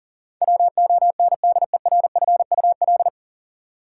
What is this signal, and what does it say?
Morse code, heard as a keyed tone, about 39 wpm, 700 Hz. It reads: WONDERFUL
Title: Sentences from Top 1100 Words - New Word Drill - 40wpm